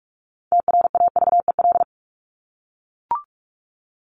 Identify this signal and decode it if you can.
Morse code, heard as a keyed tone, about 44 wpm, 700 Hz. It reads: TRAVEL
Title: Top 500 Words Repeated 45wpm